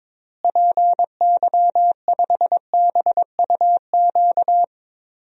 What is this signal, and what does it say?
Morse code, heard as a keyed tone, about 22 wpm, 700 Hz. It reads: PY5BUQ
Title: DXCC-100 Call Signs - 6 Characters - Repeated 3x - 22wpm